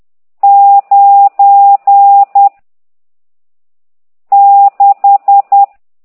Morse code, heard as a keyed tone, about 10 wpm, 800 Hz. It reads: 9 6
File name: ct2idw / exerc14